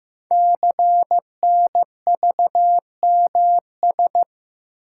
Morse code, heard as a keyed tone, about 15 wpm, 700 Hz. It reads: CNVMS
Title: Code Groups - 5 Letters - 15wpm